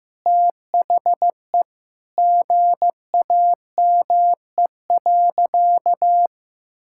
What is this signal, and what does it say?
Morse code, heard as a keyed tone, about 15 wpm, 700 Hz. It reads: THE GAME.
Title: Sentences from Top 1000 Words - Review Part 6 - 15wpm